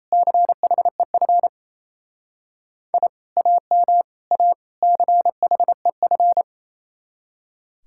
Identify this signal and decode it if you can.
Morse code, heard as a keyed tone, about 34 wpm, 700 Hz. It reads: CHEF I AM A CHEF
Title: Speed-Racing Common QSO Phrases - 28wpm